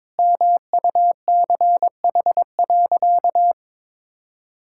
Morse code, heard as a keyed tone, about 22 wpm, 700 Hz. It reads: MUCH.
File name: practice / 22wpm / Sentences with 2-Word Phrases - 22wpm